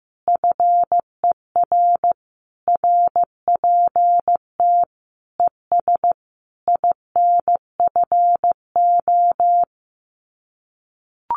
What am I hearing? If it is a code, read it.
Morse code, heard as a keyed tone, about 15 wpm, 700 Hz. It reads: FER RPT ES INFO
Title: Common QSO Phrases Encore 15wpm